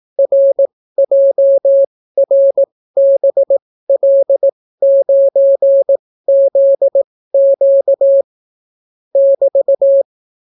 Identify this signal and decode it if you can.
Morse code, heard as a keyed tone, about 18 wpm, spaced as for 14 wpm, 550 Hz. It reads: RJRBL9ZQ <BT>